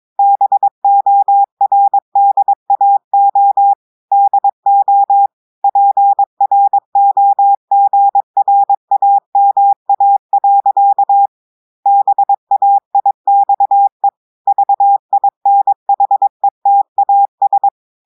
Morse code, heard as a keyed tone, about 22 wpm, 800 Hz. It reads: BORDAO DO PROGRAMA. BAIXE VINHETAS
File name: audios / portugais / portugais texte en clair 2 vitesse 22 mots minutes